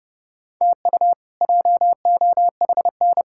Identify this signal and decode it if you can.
Morse code, heard as a keyed tone, about 30 wpm, 700 Hz. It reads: TU JOHN
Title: K1USN SST QSO Practice - 30wpm